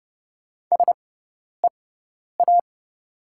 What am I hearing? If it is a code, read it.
Morse code, heard as a keyed tone, about 30 wpm, spaced as for 10 wpm, 700 Hz. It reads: SEA
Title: 6x Character Spacing - Top 300 Words - 30wpm